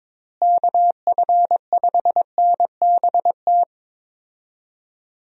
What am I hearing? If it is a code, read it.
Morse code, heard as a keyed tone, about 22 wpm, 700 Hz. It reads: KF5NBT
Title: USA Call Signs - 6 Characters - Repeated 3x - 22wpm